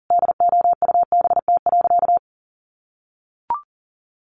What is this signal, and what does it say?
Morse code, heard as a keyed tone, about 40 wpm, 700 Hz. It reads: DOUBT.
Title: Sentences from Top 1500 Words 40wpm